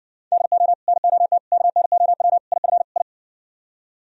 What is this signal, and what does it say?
Morse code, heard as a keyed tone, about 60 wpm, 700 Hz. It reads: DO NOT KNOW IF I